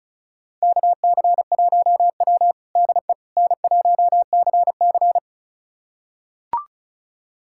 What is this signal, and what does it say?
Morse code, heard as a keyed tone, about 36 wpm, 700 Hz. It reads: KC1W DE N1CC
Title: USA Callsign DE Callsign - 4 Characters - 35wpm